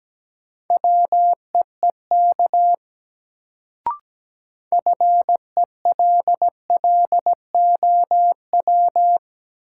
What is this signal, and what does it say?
Morse code, heard as a keyed tone, about 17 wpm, 700 Hz. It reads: WEEK FELLOW
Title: Sets of 5 Words Encore - New Word Drill - 17wpm